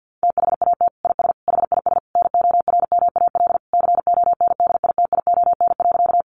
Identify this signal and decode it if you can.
Morse code, heard as a keyed tone, about 50 wpm, 700 Hz. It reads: THAT IS HIS NORMAL CONDITION.
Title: Sentences from Top 1200 Words 50wpm